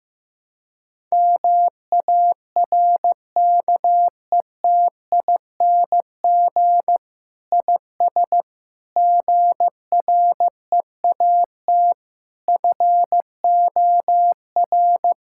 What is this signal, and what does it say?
Morse code, heard as a keyed tone, about 16 wpm, 700 Hz. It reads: MARKETING IS GREAT FOR
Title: Sentences from Top 1000 Words - Review Part 5 - 15wpm